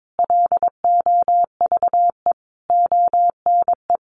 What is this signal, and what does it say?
Morse code, heard as a keyed tone, about 22 wpm, 700 Hz. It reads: LOVE ONE
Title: Sentences from Top 600 Words 22wpm